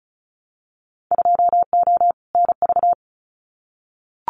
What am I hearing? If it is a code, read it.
Morse code, heard as a keyed tone, about 34 wpm, 690 Hz. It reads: 2O NV
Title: Winter Field Day Exchanges 35wpm